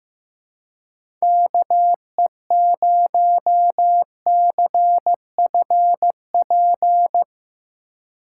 Morse code, heard as a keyed tone, about 15 wpm, 700 Hz. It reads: KE0CFP
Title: USA Call Signs - 6 Characters - 2x Character Spacing - 15wpm